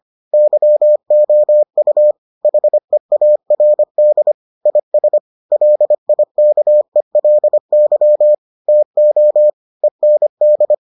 Morse code, heard as a keyed tone, about 25 wpm, 600 Hz. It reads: YOU HEARD IS LIKELY TO END